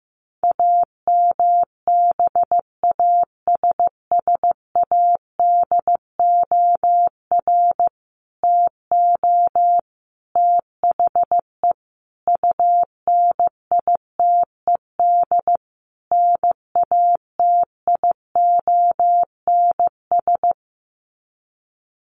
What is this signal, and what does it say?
Morse code, heard as a keyed tone, about 15 wpm, 700 Hz. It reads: AMBASSADOR TO THE UNITED NATIONS
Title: Sets of 5 Words Encore2 15wpm